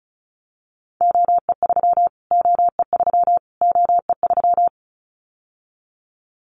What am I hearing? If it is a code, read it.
Morse code, heard as a keyed tone, about 34 wpm, 700 Hz. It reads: OE3 OE3 OE3
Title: Callsign Prefixes for Top 25 DXCC - Repeated 3x - 35wpm